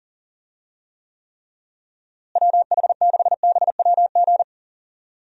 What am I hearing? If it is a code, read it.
Morse code, heard as a keyed tone, about 40 wpm, 700 Hz. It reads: WH6BWG
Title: USA Call Signs - 6 Characters - Repeated 3x - 40wpm